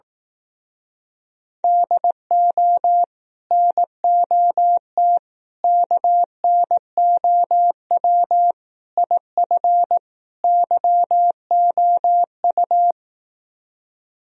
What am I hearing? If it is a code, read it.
Morse code, heard as a keyed tone, about 18 wpm, 700 Hz. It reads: DO NOT KNOW IF YOU
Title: Speed-Racing - Sets of 5 Words 15wpm